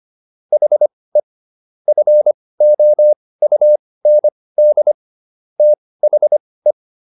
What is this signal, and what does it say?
Morse code, heard as a keyed tone, about 25 wpm, spaced as for 18 wpm, 600 Hz. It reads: HE FOUND THE